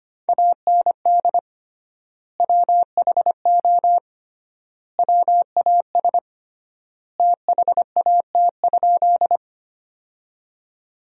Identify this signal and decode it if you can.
Morse code, heard as a keyed tone, about 25 wpm, 700 Hz. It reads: AND WHO WAS THAT?
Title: 3x Word-Spacing - Sentences from Top 100 Words - 25wpm